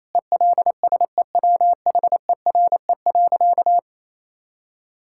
Morse code, heard as a keyed tone, about 28 wpm, 700 Hz. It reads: ELSEWHERE.
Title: Sentences from Top 1600 Words 28wpm